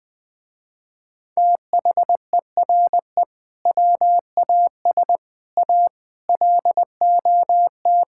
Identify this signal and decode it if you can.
Morse code, heard as a keyed tone, about 20 wpm, 700 Hz. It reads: THERE WAS A LOT
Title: Sentences with 5-Word Phrases - Spoken Beginning - 20wpm